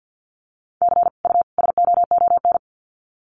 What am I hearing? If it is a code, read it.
Morse code, heard as a keyed tone, about 50 wpm, 700 Hz. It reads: C U SOON
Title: Common QSO Phrases 50wpm